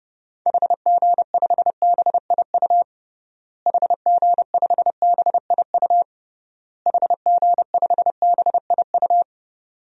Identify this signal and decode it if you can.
Morse code, heard as a keyed tone, about 30 wpm, 700 Hz. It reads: HG5BIU HG5BIU HG5BIU
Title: DXCC-100 Call Signs - 6 Characters - Repeated 3x - 30wpm